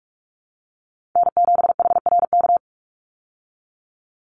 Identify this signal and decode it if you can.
Morse code, heard as a keyed tone, about 44 wpm, 700 Hz. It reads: N7HRK
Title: Call Signs Repeated 3x - 45wpm